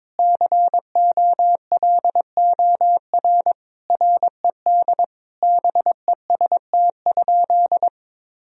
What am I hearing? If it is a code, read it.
Morse code, heard as a keyed tone, about 22 wpm, 700 Hz. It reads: COLOR RED BEST?
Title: Sentences from Top 1000 Words - Review Part 2 - 22wpm